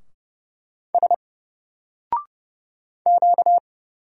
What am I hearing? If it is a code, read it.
Morse code, heard as a keyed tone, about 30 wpm, 700 Hz. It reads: S Q
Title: Single Character (Letter:Number:Punctuation:Prosign) 30wpm